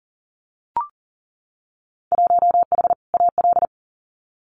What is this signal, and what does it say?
Morse code, heard as a keyed tone, about 40 wpm, 700 Hz. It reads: 1H AL
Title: Winter Field Day Exchanges 40wpm